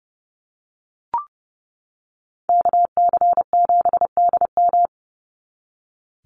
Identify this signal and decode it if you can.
Morse code, heard as a keyed tone, about 30 wpm, 700 Hz. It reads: KC7DM
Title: NA SOTA Activator Callsigns 30wpm